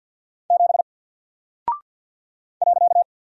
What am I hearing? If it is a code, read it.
Morse code, heard as a keyed tone, about 54 wpm, 700 Hz. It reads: 7 .
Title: Single Character (Letter:Number:Punctuation:Prosign) 50wpm